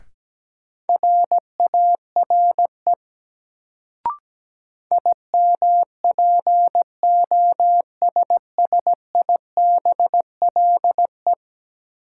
Morse code, heard as a keyed tone, about 17 wpm, 700 Hz. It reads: RARE IMPOSSIBLE
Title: Sentences from Top 1100 Words - New Word Drill - 17wpm